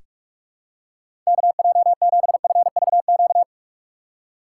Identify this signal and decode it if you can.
Morse code, heard as a keyed tone, about 42 wpm, 700 Hz. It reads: KJ7FVQ
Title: USA Call Signs - 6 Characters - 6x Character Spacing - 45wpm